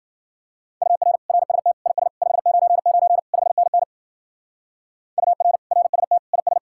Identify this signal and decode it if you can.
Morse code, heard as a keyed tone, about 65 wpm, 700 Hz. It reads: UR RST IS 599 5NN UR RST IS